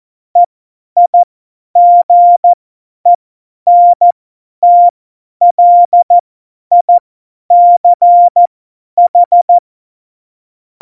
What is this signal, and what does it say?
Morse code, heard as a keyed tone, about 14 wpm, spaced as for 10 wpm, 700 Hz. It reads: EIGENTLICH
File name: Morsekurs / Witze / Witz 17 April 2018, 50 F70